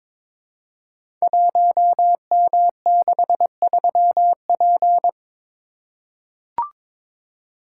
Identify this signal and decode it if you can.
Morse code, heard as a keyed tone, about 22 wpm, 700 Hz. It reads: 1M63P
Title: Code Groups - 5 Letter:Numbers - 22wpm